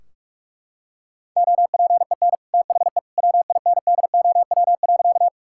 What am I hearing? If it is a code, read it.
Morse code, heard as a keyed tone, about 41 wpm, 700 Hz. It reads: OPEN THE WINDOW.
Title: Sentences from Top 1600 Words 45wpm